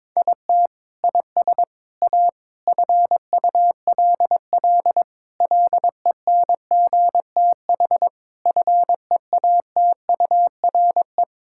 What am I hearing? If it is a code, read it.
Morse code, heard as a keyed tone, about 21 wpm, 700 Hz. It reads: IT IS A FULL LENGTH FEATURE
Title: Sentences from Top 1000 Words - Review Part 5 - 22wpm